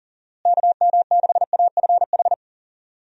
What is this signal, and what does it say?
Morse code, heard as a keyed tone, about 37 wpm, 700 Hz. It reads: KM6AFH
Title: USA Call Signs - 6 Characters - Repeated 2x - 40wpm